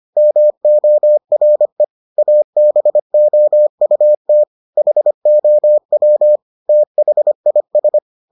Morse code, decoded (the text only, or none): MORE ABOUT HOW THIS